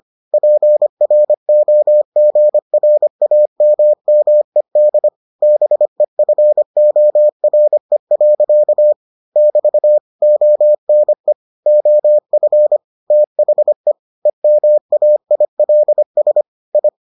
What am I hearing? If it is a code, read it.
Morse code, heard as a keyed tone, about 25 wpm, 600 Hz. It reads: PROGRAMMED BEFORE. <BT> ONE OF THE EMAILS I